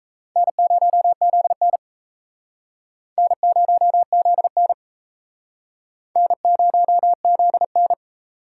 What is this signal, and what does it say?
Morse code, heard as a keyed tone, about 36 wpm, 700 Hz. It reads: N0ZN N0ZN N0ZN